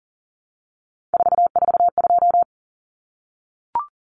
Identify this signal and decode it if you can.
Morse code, heard as a keyed tone, about 40 wpm, 700 Hz. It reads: 442
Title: Numbers - 3-Digit 40wpm